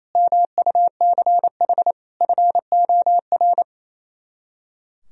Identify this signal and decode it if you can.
Morse code, heard as a keyed tone, about 28 wpm, 700 Hz. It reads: MUCH FOR
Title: Speed-Racing - Sets of 5 Words 28wpm